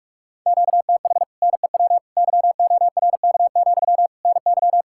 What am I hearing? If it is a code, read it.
Morse code, heard as a keyed tone, about 56 wpm, 700 Hz. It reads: QTH NEW YORK, NY
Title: Speed-Racing Common QSO Phrases - 45wpm